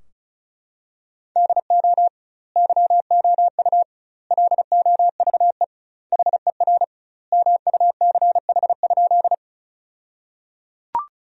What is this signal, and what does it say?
Morse code, heard as a keyed tone, about 36 wpm, 700 Hz. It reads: DO YOU LOVE HER MUCH?
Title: 2x Word-Spacing - Sentences from Top 400 Words - 35wpm